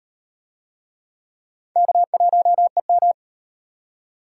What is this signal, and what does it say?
Morse code, heard as a keyed tone, about 38 wpm, 700 Hz. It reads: K1EM